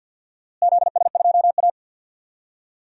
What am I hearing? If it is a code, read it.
Morse code, heard as a keyed tone, about 51 wpm, 700 Hz. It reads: ZS2A
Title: DXCC-100 Call Signs - 4 Characters - Repeated 2x - 50wpm